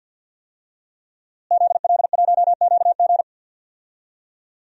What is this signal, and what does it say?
Morse code, heard as a keyed tone, about 50 wpm, 700 Hz. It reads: ZL1QG
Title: DXCC-100 Call Signs - 5 Characters - Repeated 2x - 50wpm